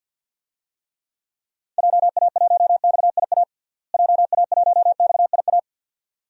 Morse code, heard as a keyed tone, about 52 wpm, 700 Hz. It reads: JA1XIA JA1XIA
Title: DXCC-100 Call Signs - 6 Characters - Repeated 2x - 50wpm